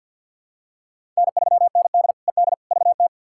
Morse code, heard as a keyed tone, about 51 wpm, 700 Hz. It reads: N3ND ED VT
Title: K1USN SST QSO Practice - 50wpm